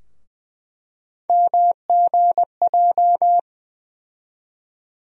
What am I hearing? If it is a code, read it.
Morse code, heard as a keyed tone, about 20 wpm, 700 Hz. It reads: MGJ
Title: USA Call Sign Postfixes - 3 Letters 20wpm